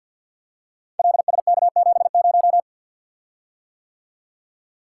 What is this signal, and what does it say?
Morse code, heard as a keyed tone, about 45 wpm, 700 Hz. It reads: LSK70
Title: Code Groups - 5 Letter:Numbers - 50wpm